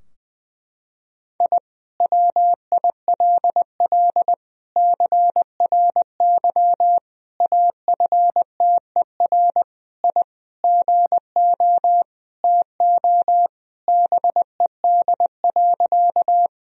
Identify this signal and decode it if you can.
Morse code, heard as a keyed tone, about 20 wpm, 700 Hz. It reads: I WILL CRY AFTER I GO TO BED.
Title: Sentences from Top 1000 Words - Review Part 2 - 20wpm